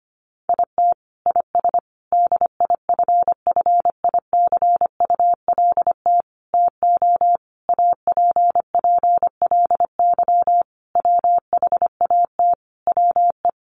Morse code, heard as a keyed tone, about 25 wpm, 700 Hz. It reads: IT IS DIFFICULT TO APPLY WHAT WE
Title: Sentences from Top 1200 Words 25wpm